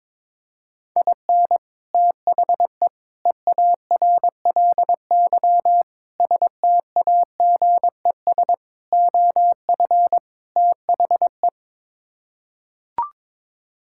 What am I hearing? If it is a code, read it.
Morse code, heard as a keyed tone, about 22 wpm, 700 Hz. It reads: IN THE EARLY STAGES OF THE
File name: practice / 22wpm / Sets of 6 Words - 22wpm